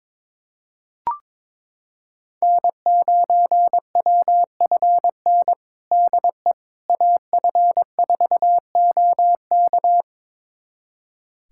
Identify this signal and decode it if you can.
Morse code, heard as a keyed tone, about 22 wpm, 700 Hz. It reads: N9WFN DE AF4OK
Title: USA Callsign DE Callsign - 5 Characters - 22wpm